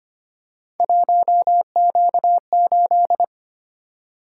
Morse code, heard as a keyed tone, about 25 wpm, 700 Hz. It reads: 1Q8
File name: Three Letter-Number Combinations - 25wpm